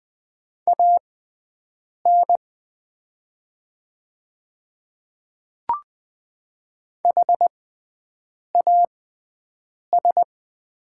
Morse code, heard as a keyed tone, about 20 wpm, spaced as for 7 wpm, 700 Hz. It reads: AN HAS
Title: 6x Character Spacing - Top 100 Words - 20wpm